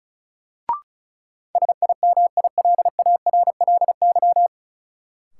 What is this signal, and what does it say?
Morse code, heard as a keyed tone, about 36 wpm, 700 Hz. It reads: SIMILARLY
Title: Sentences from Top 1400 Words - New Word Drill - 35wpm